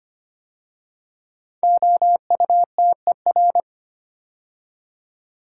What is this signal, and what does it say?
Morse code, heard as a keyed tone, about 25 wpm, 700 Hz. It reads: OUTER
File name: 5 Letter Words - First 3 Letters Given - 25wpm